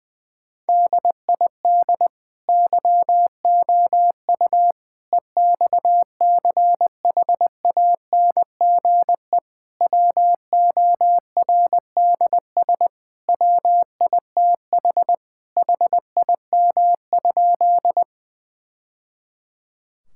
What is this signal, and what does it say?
Morse code, heard as a keyed tone, about 20 wpm, 700 Hz. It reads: DID YOU EXCHANGE WORDS WITH HIM?
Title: Sentences from Top 1000 Words - Review Part 6 - 20wpm